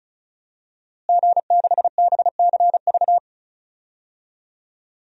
Morse code, heard as a keyed tone, about 35 wpm, 700 Hz. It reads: G6BCV
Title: DXCC-100 Call Signs - 5 Characters - Repeated 3x - 35wpm